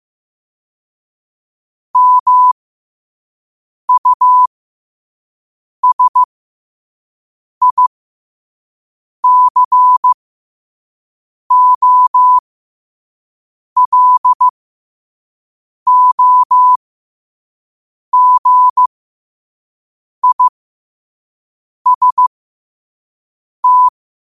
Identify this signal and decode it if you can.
Morse code, heard as a keyed tone, about 15 wpm, spaced as for 5 wpm, 1000 Hz. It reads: MUSICOLOGIST